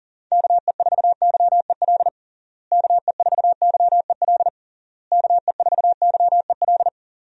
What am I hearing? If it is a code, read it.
Morse code, heard as a keyed tone, about 37 wpm, 700 Hz. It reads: KE4YEL KE4YEL KE4YEL
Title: USA Call Signs - 6 Characters - Repeated 3x - 40wpm